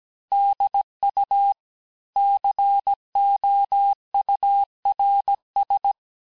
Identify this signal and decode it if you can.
Morse code, heard as a keyed tone, about 17 wpm, 800 Hz. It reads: DU COURS
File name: cours-du-lundi files / cours-1587-texte-chiffre-vitesse-16MPM